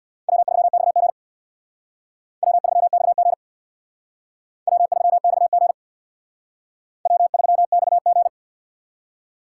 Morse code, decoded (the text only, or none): W3XG W3XG W3XG W3XG